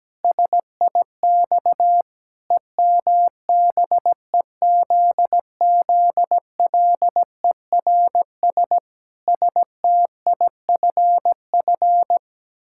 Morse code, heard as a keyed tone, about 17 wpm, 700 Hz. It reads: SIX EMBEZZLERS STIFF
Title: Pangram Challenge v2 - 17wpm